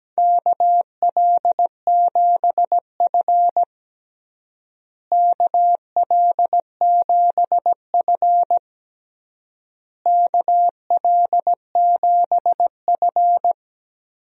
Morse code, heard as a keyed tone, about 17 wpm, 700 Hz. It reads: KL7F KL7F KL7F
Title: DXCC-100 Call Signs - 4 Characters - Repeated 3x - 17wpm